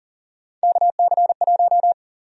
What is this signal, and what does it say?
Morse code, heard as a keyed tone, about 38 wpm, 700 Hz. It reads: KC1
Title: USA Call Sign Prefixes - Repeated 2x - 40wpm